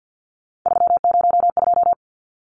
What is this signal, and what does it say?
Morse code, heard as a keyed tone, about 50 wpm, 700 Hz. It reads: 302
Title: Numbers - 3-Digit 50wpm